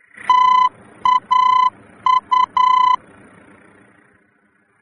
Morse code, heard as a keyed tone, about 10 wpm, 1010 Hz. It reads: TAU
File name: germany / mx / TAU-D-VOR